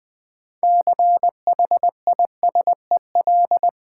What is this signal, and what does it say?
Morse code, heard as a keyed tone, about 21 wpm, 700 Hz. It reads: CHISEL
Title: Difficult Words with Hints - 20wpm